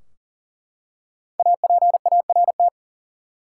Morse code, heard as a keyed tone, about 39 wpm, 700 Hz. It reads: APART
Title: Five Letter Words - 40wpm